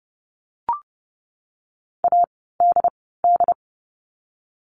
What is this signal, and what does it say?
Morse code, heard as a keyed tone, about 30 wpm, 700 Hz. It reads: A D D
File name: Lesson 16f - Letter c - Words 30wpm